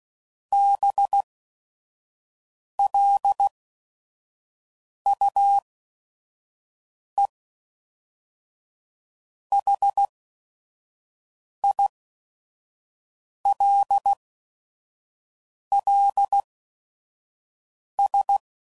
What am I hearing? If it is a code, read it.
Morse code, heard as a keyed tone, about 16 wpm, spaced as for 5 wpm, 800 Hz. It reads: BLUE HILLS
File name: CW Code Groups @5WPM / fromSM's / Cities